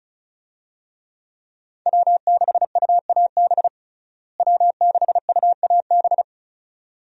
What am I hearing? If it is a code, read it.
Morse code, heard as a keyed tone, about 34 wpm, 700 Hz. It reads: W6UAB W6UAB
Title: USA Call Signs - 5 Characters - Repeated 2x - 35wpm